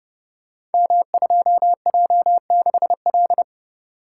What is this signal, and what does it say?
Morse code, heard as a keyed tone, about 29 wpm, 700 Hz. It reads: M2J6L
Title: Code Groups - 5 Letter:Numbers - 30wpm